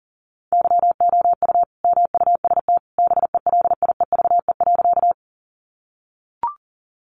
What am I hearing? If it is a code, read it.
Morse code, heard as a keyed tone, about 40 wpm, 700 Hz. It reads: YOU MUST BELIEVE.
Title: Sentences from Top 1000 Words 40wpm